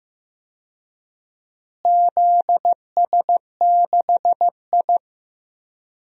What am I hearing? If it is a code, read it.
Morse code, heard as a keyed tone, about 16 wpm, 700 Hz. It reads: ZS6I
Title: DXCC-100 Call Signs - 4 Characters - Repeated 3x - 15wpm